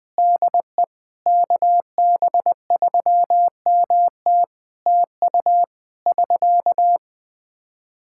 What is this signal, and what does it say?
Morse code, heard as a keyed tone, about 20 wpm, 700 Hz. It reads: DE KB3MT TU <SK>
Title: Practice QSO - Exchange 4 - Ending - 20wpm